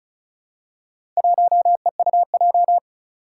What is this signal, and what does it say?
Morse code, heard as a keyed tone, about 36 wpm, 700 Hz. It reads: 1EUJ
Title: Four Challenging Letter-Number Combinations 35wpm